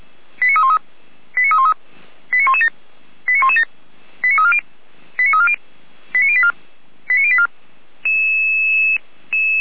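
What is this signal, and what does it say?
Five-tone selective call (ZVEI), heard as quick runs of five tones, tones of about 70 ms. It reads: 89313 89313 89117 89117 89340 89340 89084 89084
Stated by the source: ZVEI five-tone call 89310 received by SDR.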